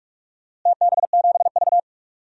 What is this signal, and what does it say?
Morse code, heard as a keyed tone, about 46 wpm, 700 Hz. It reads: TB7V
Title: DXCC-100 Call Signs - 4 Characters - Repeated 3x - 45wpm